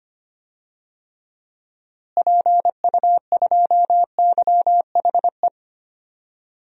Morse code, heard as a keyed tone, about 25 wpm, 700 Hz. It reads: PU2YHE
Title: DXCC-100 Call Signs - 6 Characters - Repeated 2x - 25wpm